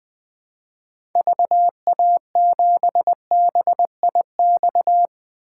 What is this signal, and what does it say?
Morse code, heard as a keyed tone, about 20 wpm, 700 Hz. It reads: VA7BIX
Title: 2023 POTA Activator Call Signs - 20wpm